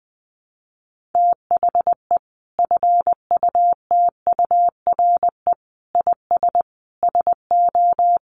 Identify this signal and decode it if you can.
Morse code, heard as a keyed tone, about 20 wpm, 700 Hz. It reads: THE FUTURE IS SO
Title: Sentences from Top 700 Words 20wpm